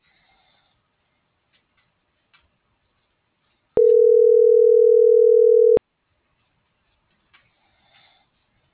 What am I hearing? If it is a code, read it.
Ringback tone (North America), 440 + 480 Hz.